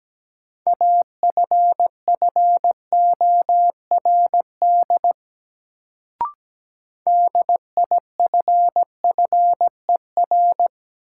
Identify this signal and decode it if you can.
Morse code, heard as a keyed tone, about 17 wpm, 700 Hz. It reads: AFFORD DIFFER
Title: Sentences from Top 1100 Words - New Word Drill Alt - 17wpm